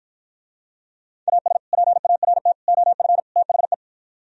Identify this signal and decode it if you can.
Morse code, heard as a keyed tone, about 52 wpm, 700 Hz. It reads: AS PART OF THE